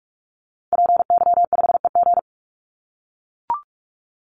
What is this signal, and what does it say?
Morse code, heard as a keyed tone, about 44 wpm, 700 Hz. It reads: PY5EG
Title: Call Signs - 45wpm